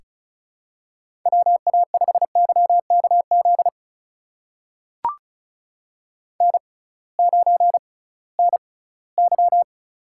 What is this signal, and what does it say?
Morse code, heard as a keyed tone, about 36 wpm, spaced as for 21 wpm, 700 Hz. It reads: WA5YKZ N9NY
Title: USA Call Signs - 6x Character Spacing - 35wpm